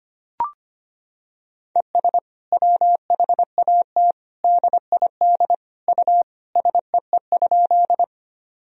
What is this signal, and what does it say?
Morse code, heard as a keyed tone, about 25 wpm, 700 Hz. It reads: ES WHAT DID U SEE?
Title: Sentences from Top 100 Words and Abbreviations 25wpm